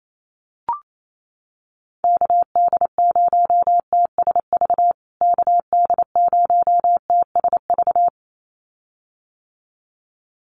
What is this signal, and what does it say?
Morse code, heard as a keyed tone, about 28 wpm, 700 Hz. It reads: KD0TSV KD0TSV
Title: USA Call Signs Repeated 2x 28wpm